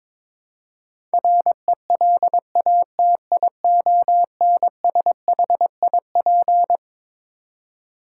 Morse code, heard as a keyed tone, about 22 wpm, 700 Hz. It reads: RELATIONSHIP
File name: Twelve Letter Words - 22wpm